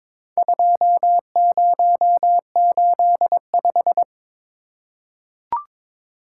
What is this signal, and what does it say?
Morse code, heard as a keyed tone, about 22 wpm, 700 Hz. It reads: 2085
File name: practice / 22wpm / Four Letter-Number Combinations - 22wpm